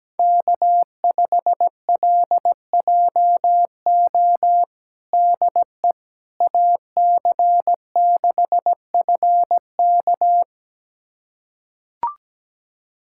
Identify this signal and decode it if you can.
Morse code, heard as a keyed tone, about 17 wpm, 700 Hz. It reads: K5LJO DE AC6FK
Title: USA Callsign DE Callsign - 5 Characters - 17wpm